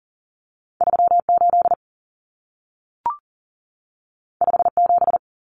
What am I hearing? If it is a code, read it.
Morse code, heard as a keyed tone, about 40 wpm, 700 Hz. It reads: 38 57
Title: Numbers - 2-Digit 40wpm